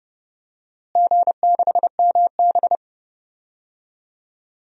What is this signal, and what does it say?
Morse code, heard as a keyed tone, about 32 wpm, 700 Hz. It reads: G6MB
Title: DXCC-100 Call Signs - 4 Characters - Repeated 2x - 30wpm